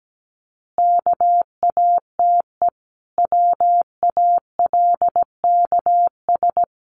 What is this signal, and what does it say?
Morse code, heard as a keyed tone, about 17 wpm, 700 Hz. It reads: KATE WALKS
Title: Easy Phrases 17wpm